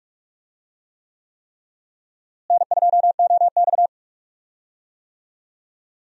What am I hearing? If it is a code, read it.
Morse code, heard as a keyed tone, about 46 wpm, 700 Hz. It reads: N2OX
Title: USA Call Signs - 4 Characters - Repeated 3x - 45wpm